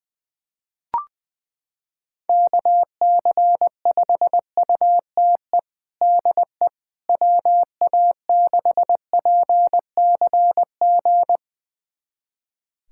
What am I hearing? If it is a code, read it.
Morse code, heard as a keyed tone, about 21 wpm, 700 Hz. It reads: KC5UTE DE WA6PCG
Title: USA Callsign DE Callsign 20wpm